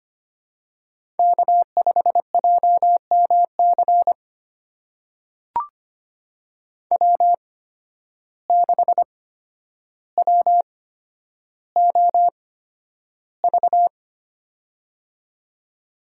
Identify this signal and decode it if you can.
Morse code, heard as a keyed tone, about 25 wpm, spaced as for 11 wpm, 700 Hz. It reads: K5JMC W6WOV
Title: USA Call Signs - 5 Characters - 8x Character Spacing - 25wpm